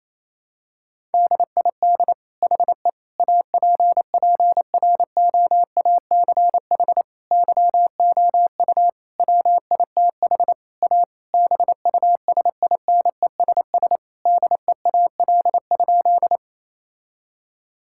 Morse code, heard as a keyed tone, about 28 wpm, 700 Hz. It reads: DID HE APPROACH YOU WITH A BUSINESS DEAL?
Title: Sentences from Top 1000 Words - Review Part 6 - 28wpm